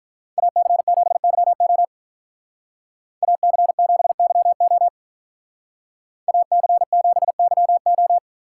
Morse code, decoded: AC7YO AC7YO AC7YO

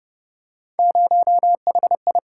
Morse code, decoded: 0HI

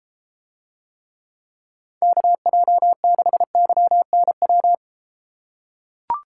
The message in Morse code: KJ6YNW

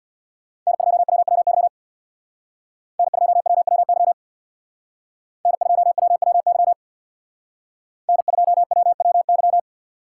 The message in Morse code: N2WWY N2WWY N2WWY N2WWY

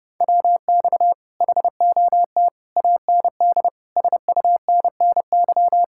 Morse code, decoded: WX HOT AND SUNNY

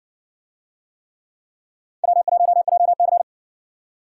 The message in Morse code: W1JC

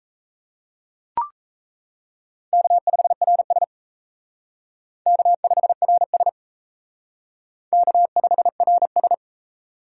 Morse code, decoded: K5RS K5RS K5RS

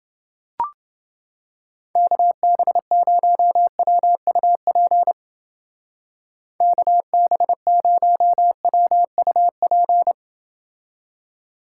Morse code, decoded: KB0WUP KB0WUP

